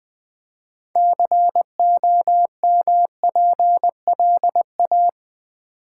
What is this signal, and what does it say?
Morse code, heard as a keyed tone, about 20 wpm, 700 Hz. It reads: COMPLA
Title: Build A Word - 2 Letters At A Time - 8-Letter Words - 20wpm